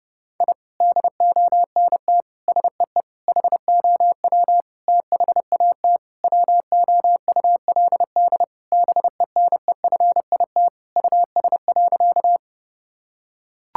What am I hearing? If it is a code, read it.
Morse code, heard as a keyed tone, about 30 wpm, 700 Hz. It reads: I DONT SEE HOW THAT WOULD BENEFIT US.
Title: Sentences from Top 1000 Words - Review Part 6 - 30wpm